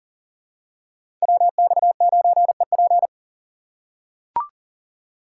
Morse code, heard as a keyed tone, about 37 wpm, 700 Hz. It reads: WX9EP
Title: USA Call Signs - 5 Characters - 4x Character Spacing - 40wpm